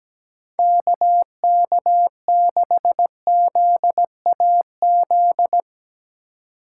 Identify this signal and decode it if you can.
Morse code, heard as a keyed tone, about 17 wpm, 700 Hz. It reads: KK6ZAZ
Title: USA Call Signs - 4x Character Spacing - 17wpm